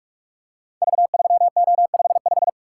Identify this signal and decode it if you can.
Morse code, heard as a keyed tone, about 44 wpm, 700 Hz. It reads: V3O55
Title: Code Groups - 5 Letter:Numbers - 45wpm